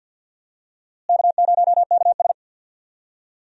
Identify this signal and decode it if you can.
Morse code, heard as a keyed tone, about 47 wpm, 700 Hz. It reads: K0KS
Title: USA Call Signs - 4 Characters - Repeated 2x - 50wpm